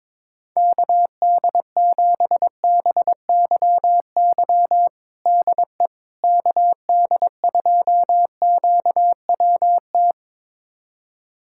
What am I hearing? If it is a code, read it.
Morse code, heard as a keyed tone, about 22 wpm, 700 Hz. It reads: KD7BYY DE KD2QWT
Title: USA Callsign DE Callsign - 6 Characters - 22wpm